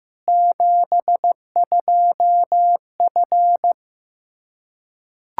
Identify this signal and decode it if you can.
Morse code, heard as a keyed tone, about 15 wpm, 700 Hz. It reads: 72F
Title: Practice QSO - Exchange 2 - Info - 15wpm